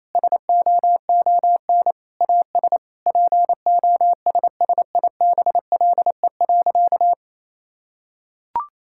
Morse code, heard as a keyed tone, about 28 wpm, 700 Hz. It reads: SOON AS POSSIBLE.
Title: Sentences from Top 1000 Words - Review Part 5 - 28wpm